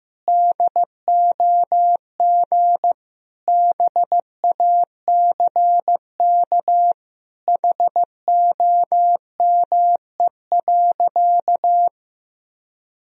Morse code, heard as a keyed tone, about 16 wpm, 700 Hz. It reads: DOG BACK HOME.
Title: Sentences from Top 1000 Words - Review Part 2 - 15wpm